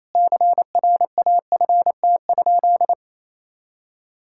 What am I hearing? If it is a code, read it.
Morse code, heard as a keyed tone, about 28 wpm, 700 Hz. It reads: CRAFT?
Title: Sentences from Top 1000 Words - Review Part 6 - 28wpm